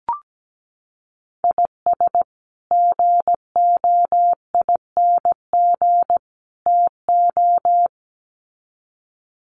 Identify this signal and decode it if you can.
Morse code, heard as a keyed tone, about 17 wpm, 700 Hz. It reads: IS GOING TO
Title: Sets of 3 Words 17wpm